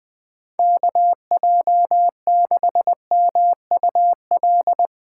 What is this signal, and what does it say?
Morse code, heard as a keyed tone, about 20 wpm, 700 Hz. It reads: KJ6MUL
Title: Speed Racing - USA 2x3 Call Signs 17wpm